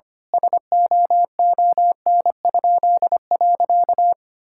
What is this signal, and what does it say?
Morse code, heard as a keyed tone, about 25 wpm, 700 Hz. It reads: SOON?.
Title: Pangram Challenge v2 - 25wpm